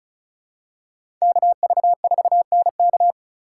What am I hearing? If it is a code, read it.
Morse code, heard as a keyed tone, about 36 wpm, 700 Hz. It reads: KV4NK
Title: USA Call Signs - 5 Characters - 2x Character Spacing - 35wpm